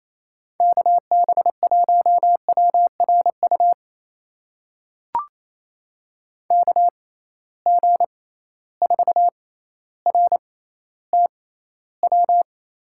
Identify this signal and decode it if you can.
Morse code, heard as a keyed tone, about 28 wpm, spaced as for 14 wpm, 700 Hz. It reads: KB1WRU KG4RTW
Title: USA Call Signs - 6 Characters - 6x Character Spacing - 28wpm